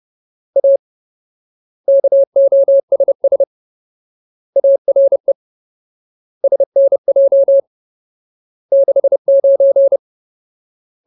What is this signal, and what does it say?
Morse code, heard as a keyed tone, about 30 wpm, 550 Hz. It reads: A KOSS ARE SNJ 69
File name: Ultimate Word List / 30WPM Ultimate